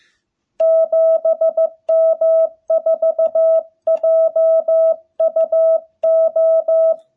Morse code, heard as a keyed tone, about 15 wpm, 660 Hz. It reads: 7M4JUO
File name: jh8cbh / 2021cw